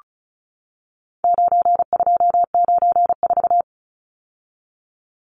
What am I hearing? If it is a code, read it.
Morse code, heard as a keyed tone, about 34 wpm, 700 Hz. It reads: 9294
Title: Numbers - 4-Digit 35wpm